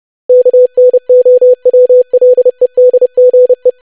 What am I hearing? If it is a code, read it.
Morse code, heard as a keyed tone, about 30 wpm, 500 Hz. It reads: KNOWLEDGE
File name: lesson / web / longwords-25wpm